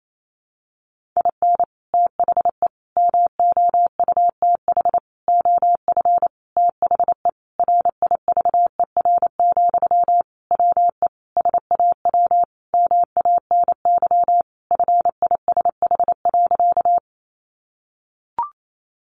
Morse code, heard as a keyed tone, about 28 wpm, 700 Hz. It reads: IN THE MOUTH OF THE RIVER, WE SAW MANY FISH.
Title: Sentences from Top 1500 Words 28wpm